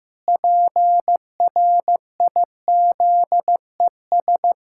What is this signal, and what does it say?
Morse code, heard as a keyed tone, about 15 wpm, 700 Hz. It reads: PRIZES
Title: Pangram Challenge v2 - 15wpm